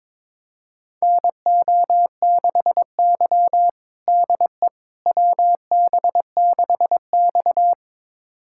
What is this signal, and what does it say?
Morse code, heard as a keyed tone, about 22 wpm, 700 Hz. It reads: NO6Y DE WB6X
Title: USA Callsign DE Callsign - 4 Characters - 22wpm